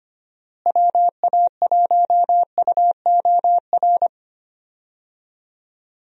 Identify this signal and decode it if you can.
Morse code, heard as a keyed tone, about 25 wpm, 700 Hz. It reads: WA1UOR
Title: USA Call Signs - 6 Characters - Repeated 3x - 25wpm